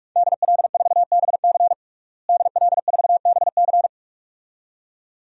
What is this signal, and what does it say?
Morse code, heard as a keyed tone, about 46 wpm, 700 Hz. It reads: DL4BC DL4BC
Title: DXCC-100 Call Signs - 5 Characters - Repeated 2x - 45wpm